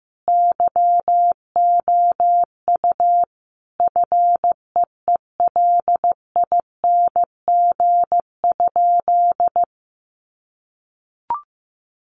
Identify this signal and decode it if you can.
Morse code, heard as a keyed tone, about 15 wpm, 700 Hz. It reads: YOU FEELING?
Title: Sentences from Top 1200 Words 15wpm